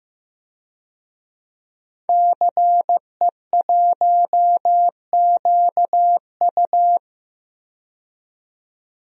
Morse code, heard as a keyed tone, about 15 wpm, 700 Hz. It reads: CE1QU
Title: DXCC-100 Call Signs - 5 Characters - 15wpm